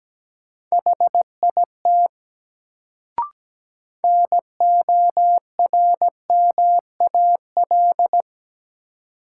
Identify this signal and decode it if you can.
Morse code, heard as a keyed tone, about 17 wpm, 700 Hz. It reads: HIT NORMAL
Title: Sentences from Top 1200 Words - New Word Drill - 17wpm